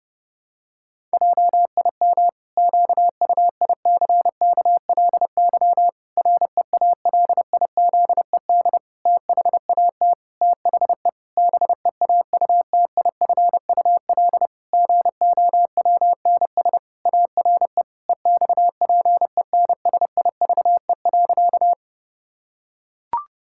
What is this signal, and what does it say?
Morse code, heard as a keyed tone, about 30 wpm, 700 Hz. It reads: JIM QUICKLY REALIZED THAT THE BEAUTIFUL GOWNS ARE EXPENSIVE.
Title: Pangram Challenge v2 - 30wpm